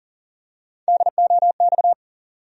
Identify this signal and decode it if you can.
Morse code, heard as a keyed tone, about 40 wpm, 700 Hz. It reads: DOX
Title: USA Call Sign Postfixes - 3 Letters 40wpm